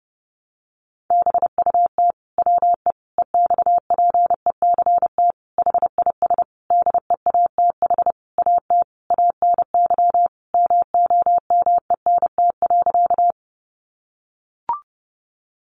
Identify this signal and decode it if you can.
Morse code, heard as a keyed tone, about 30 wpm, 700 Hz. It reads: BUT WE EXPECT HIS DEATH AT ANY MOMENT.
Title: Sentences from Top 1100 Words 30wpm